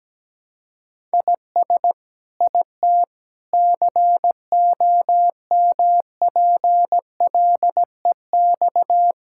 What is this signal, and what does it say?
Morse code, heard as a keyed tone, about 17 wpm, 700 Hz. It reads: IS IT COMPLEX
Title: Sentences from Top 1000 Words - Review Part 6 - 17wpm